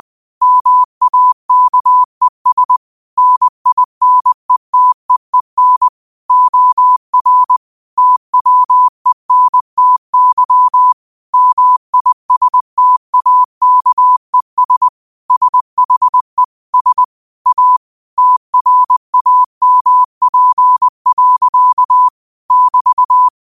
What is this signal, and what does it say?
Morse code, heard as a keyed tone, about 20 wpm, 1000 Hz. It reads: MAKES NINETEEN OR TWENTY MISTAKES SHE S A TRAMP. <BT>
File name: qotdfiles / 20wpm